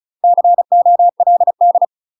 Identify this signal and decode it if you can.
Morse code, heard as a keyed tone, about 36 wpm, 700 Hz. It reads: COLD
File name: UK MASTER QSOs TRAINING / 35WPM / UK MASTER QSOs 0029